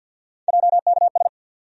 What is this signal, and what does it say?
Morse code, heard as a keyed tone, about 48 wpm, 700 Hz. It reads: JKS
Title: USA Call Sign Postfixes - 3 Letters 50wpm